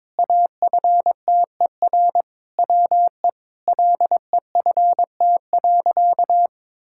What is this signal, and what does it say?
Morse code, heard as a keyed tone, about 21 wpm, 700 Hz. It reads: AFTER WE LEFT.
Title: Sentences from Top 1000 Words - Review Part 6 - 22wpm